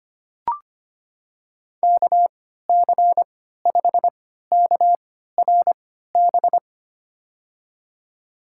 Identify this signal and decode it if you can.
Morse code, heard as a keyed tone, about 25 wpm, spaced as for 14 wpm, 700 Hz. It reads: KC5KRB
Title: USA Call Signs - 6 Characters - 3x Character Spacing - 25wpm